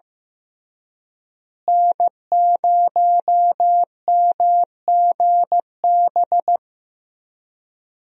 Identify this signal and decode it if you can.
Morse code, heard as a keyed tone, about 15 wpm, 700 Hz. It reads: N0MGB
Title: USA Call Signs - 5 Characters - Repeated 3x - 15wpm